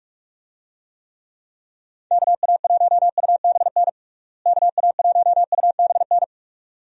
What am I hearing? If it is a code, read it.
Morse code, heard as a keyed tone, about 46 wpm, 700 Hz. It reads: KA1UBN KA1UBN
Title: USA Call Signs - 6 Characters - Repeated 2x - 45wpm